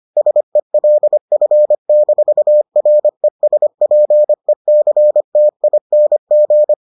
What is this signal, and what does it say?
Morse code, heard as a keyed tone, about 25 wpm, 600 Hz. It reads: SELF-RESPECTING